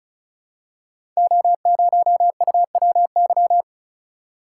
Morse code, heard as a keyed tone, about 36 wpm, 690 Hz. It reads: O0UWY
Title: Code Groups - 5 Letter:Numbers - 35wpm